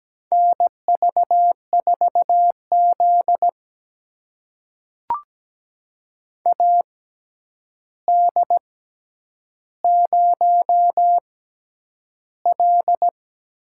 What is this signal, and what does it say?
Morse code, heard as a keyed tone, about 17 wpm, spaced as for 9 wpm, 700 Hz. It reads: NV4Z AD0L
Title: USA Call Signs - 4 Characters - 6x Character Spacing - 17wpm